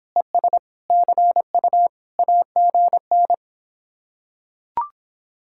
Morse code, heard as a keyed tone, about 26 wpm, 700 Hz. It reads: ES CU AGN